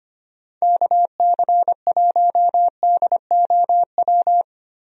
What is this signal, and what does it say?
Morse code, heard as a keyed tone, about 25 wpm, 700 Hz. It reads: KC1DOW
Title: USA Call Signs - 6 Characters - 25wpm